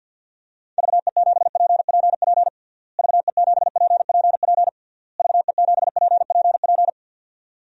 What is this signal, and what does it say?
Morse code, heard as a keyed tone, about 52 wpm, 700 Hz. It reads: VE7PPP VE7PPP VE7PPP
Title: DXCC-100 Call Signs - 6 Characters - Repeated 3x - 50wpm